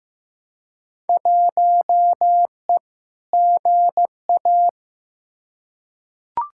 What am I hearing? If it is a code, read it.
Morse code, heard as a keyed tone, about 15 wpm, 700 Hz. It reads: 1E GA
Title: Field Day Exchanges - 15wpm